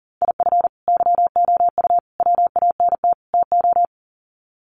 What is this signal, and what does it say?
Morse code, heard as a keyed tone, about 40 wpm, 700 Hz. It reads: IF YOU WANT TO
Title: Sets of 4 Words 40wpm